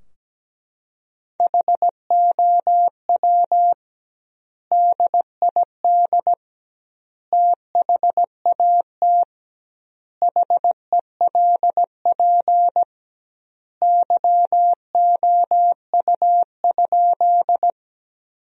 Morse code, heard as a keyed tone, about 17 wpm, 700 Hz. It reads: HOW DID THAT HELP YOU?
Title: 2x Word-Spacing - Sentences from Top 200 Words - 17wpm